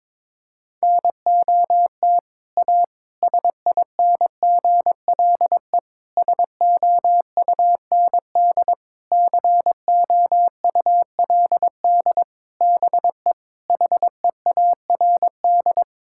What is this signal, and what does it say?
Morse code, heard as a keyed tone, about 22 wpm, 700 Hz. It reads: NOT A SINGLE SOUND COULD BE HEARD
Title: Sentences from Top 1000 Words - Review Part 5 - 22wpm